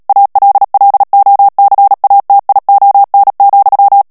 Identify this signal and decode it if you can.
Morse code, heard as a keyed tone, about 37 wpm, 800 Hz. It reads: ALLOCATION,
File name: morse mp3 / ANGLAIS texte clair 3 vitesse 35 mots minute